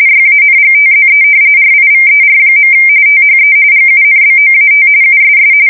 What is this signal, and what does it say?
RTTY, about 45.45 baud. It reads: YANGTZE RIVER. THE GUNBOAT USS
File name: fsk170 / usnavy4